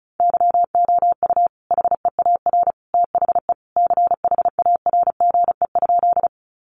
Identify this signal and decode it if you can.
Morse code, heard as a keyed tone, about 34 wpm, 700 Hz. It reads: YOU HEAR THE CHARGE?
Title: Sentences from Top 900 Words 35wpm